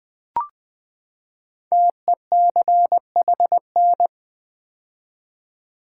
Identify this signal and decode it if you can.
Morse code, heard as a keyed tone, about 20 wpm, 700 Hz. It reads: TECHN
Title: Build A Word - 5 Letters At A Time - 10-Letter Words - 20wpm